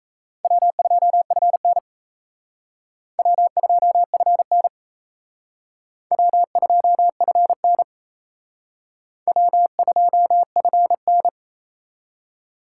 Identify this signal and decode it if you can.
Morse code, heard as a keyed tone, about 34 wpm, 700 Hz. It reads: W2FN W2FN W2FN W2FN
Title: Speed Racing - USA Call Signs 28wpm